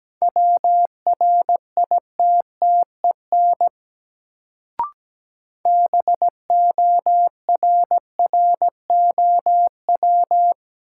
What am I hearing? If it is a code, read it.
Morse code, heard as a keyed tone, about 17 wpm, 700 Hz. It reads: WRITTEN BORROW
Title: Sentences from Top 1500 Words - New Word Drill Alt - 17wpm